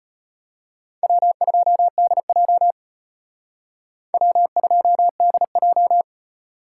Morse code, heard as a keyed tone, about 32 wpm, 700 Hz. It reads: W2DJ W2DJ